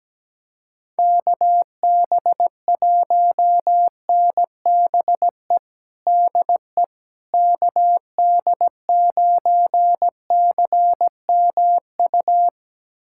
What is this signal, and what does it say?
Morse code, heard as a keyed tone, about 17 wpm, 700 Hz. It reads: KB1NBE DE KD9CMU
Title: USA Callsign DE Callsign - 6 Characters - 17wpm